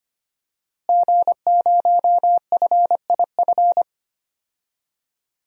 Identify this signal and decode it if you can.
Morse code, heard as a keyed tone, about 25 wpm, 700 Hz. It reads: G0FIF
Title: DXCC-100 Call Signs - 5 Characters - Repeated 3x - 25wpm